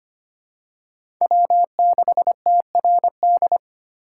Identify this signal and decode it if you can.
Morse code, heard as a keyed tone, about 25 wpm, 700 Hz. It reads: W6TRD
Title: USA Call Signs - 5 Characters - 2x Character Spacing - 25wpm